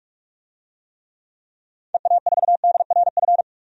Morse code, heard as a keyed tone, about 42 wpm, 700 Hz. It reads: EA4DRF
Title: DXCC-100 Call Signs - 6 Characters - 45wpm